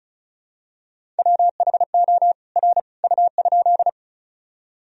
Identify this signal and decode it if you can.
Morse code, heard as a keyed tone, about 34 wpm, 700 Hz. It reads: WHO R U?
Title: Sentences from Top 100 Words and Abbreviations 35wpm